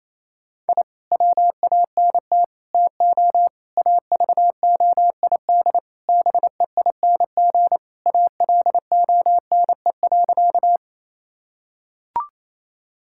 Morse code, heard as a keyed tone, about 28 wpm, 700 Hz. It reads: I WANT TO AVOID BEING ALONE.
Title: Sentences from Top 1000 Words - Review Part 6 - 28wpm